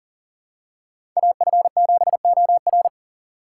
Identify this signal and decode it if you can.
Morse code, heard as a keyed tone, about 39 wpm, 700 Hz. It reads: AF7OR
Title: USA Call Signs - 5 Characters - 2x Character Spacing - 40wpm